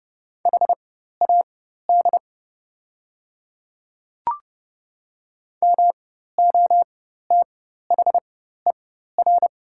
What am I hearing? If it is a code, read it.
Morse code, heard as a keyed tone, about 30 wpm, spaced as for 14 wpm, 690 Hz. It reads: HAD MOTHER
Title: 4x Character Spacing - Top 300 Words - 30wpm